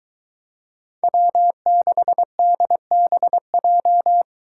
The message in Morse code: W6DBJ

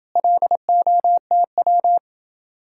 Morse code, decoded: LOTW

